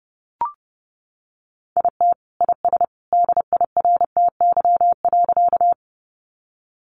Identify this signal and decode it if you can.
Morse code, heard as a keyed tone, about 30 wpm, 700 Hz. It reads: IT IS DIRTY.
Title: Additional Easy Phrases 30wpm